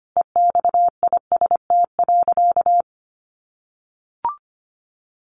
Morse code, heard as a keyed tone, about 25 wpm, 700 Hz. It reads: EXIST.
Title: Sentences from Top 1400 Words 25wpm